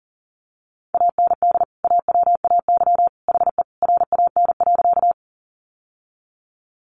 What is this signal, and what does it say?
Morse code, heard as a keyed tone, about 40 wpm, 700 Hz. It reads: AND AWAY HE RAN.
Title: Sentences from Top 600 Words 40wpm